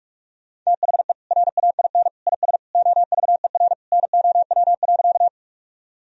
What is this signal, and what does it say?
Morse code, heard as a keyed tone, about 46 wpm, 700 Hz. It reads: THE RAIN IS OVER NOW.
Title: Sentences from Top 1000 Words - Review Part 1 - 45wpm